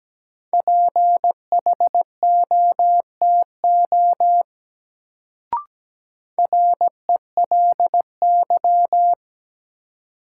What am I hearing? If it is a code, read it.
Morse code, heard as a keyed tone, about 17 wpm, 700 Hz. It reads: PHOTO RELY
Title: Sentences from Top 1200 Words - New Word Drill - 17wpm